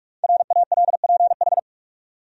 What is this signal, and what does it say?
Morse code, heard as a keyed tone, about 46 wpm, 700 Hz. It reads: RALPH
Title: USA Common Names - Max 5 Letters - 45wpm